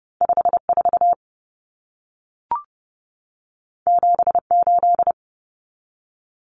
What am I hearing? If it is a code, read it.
Morse code, heard as a keyed tone, about 30 wpm, 700 Hz. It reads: 54 78
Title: Numbers - 2-Digit 30wpm